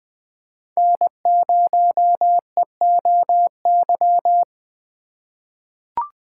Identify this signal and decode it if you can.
Morse code, heard as a keyed tone, about 20 wpm, 700 Hz. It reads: N0EOY
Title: USA Call Signs - 5 Characters - 3x Character Spacing - 20wpm